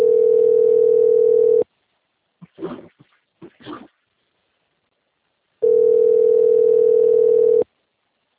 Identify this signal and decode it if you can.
Ringback tone (North America), 440 + 480 Hz.